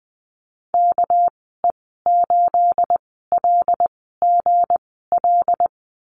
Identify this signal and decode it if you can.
Morse code, heard as a keyed tone, about 20 wpm, spaced as for 15 wpm, 700 Hz. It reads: KE8LGL
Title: 245 Lesson 39g - Letter z - Call Signs 20wpm